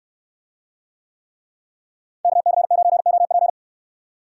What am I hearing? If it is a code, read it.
Morse code, heard as a keyed tone, about 68 wpm, 700 Hz. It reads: KC9OP